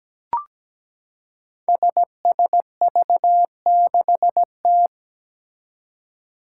Morse code, heard as a keyed tone, about 17 wpm, 700 Hz. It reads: SSV6T
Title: Five Challenging Letter-Number Combinations 17wpm